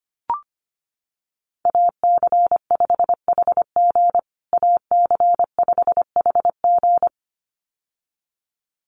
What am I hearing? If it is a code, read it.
Morse code, heard as a keyed tone, about 25 wpm, 700 Hz. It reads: AC5HG AC5HG
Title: USA Call Signs Repeated 2x 25wpm